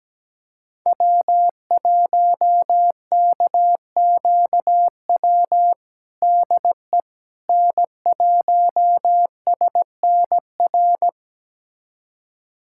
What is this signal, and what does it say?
Morse code, heard as a keyed tone, about 17 wpm, 700 Hz. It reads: W1KQW DE N1SNR
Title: USA Callsign DE Callsign - 5 Characters - 17wpm